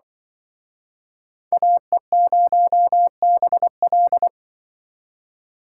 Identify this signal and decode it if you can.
Morse code, heard as a keyed tone, about 25 wpm, 700 Hz. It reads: AE0BL
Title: Speed Racing - USA 2x2 Call Signs 20wpm